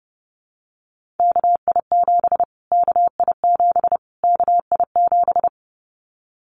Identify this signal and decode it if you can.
Morse code, heard as a keyed tone, about 30 wpm, 700 Hz. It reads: KI7 KI7 KI7
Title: Callsign Prefixes for Top 25 DXCC - Repeated 3x - 30wpm